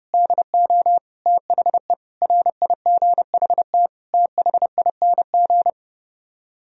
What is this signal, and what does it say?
Morse code, heard as a keyed tone, about 30 wpm, 700 Hz. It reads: DO THE RIGHT THING